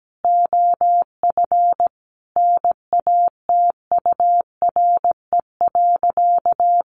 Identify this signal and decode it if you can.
Morse code, heard as a keyed tone, about 17 wpm, 700 Hz. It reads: OF NATURE.
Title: Sentences from Top 1200 Words 17wpm